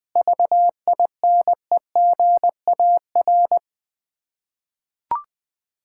Morse code, heard as a keyed tone, about 21 wpm, 690 Hz. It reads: VINEGAR
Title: Difficult Words with Hints - 20wpm